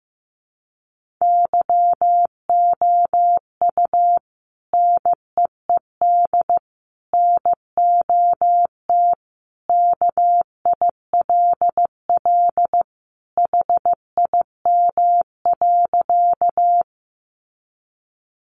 Morse code, heard as a keyed tone, about 14 wpm, 700 Hz. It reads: YOU NEED NOT KILL HIM.
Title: Sentences from Top 1400 Words 15wpm